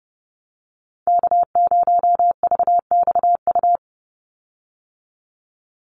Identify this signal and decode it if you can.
Morse code, heard as a keyed tone, about 30 wpm, 700 Hz. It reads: K0VXU
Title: Call Signs Repeated 3x - 30wpm